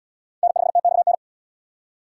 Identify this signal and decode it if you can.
Morse code, heard as a keyed tone, about 69 wpm, 700 Hz. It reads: N5TXN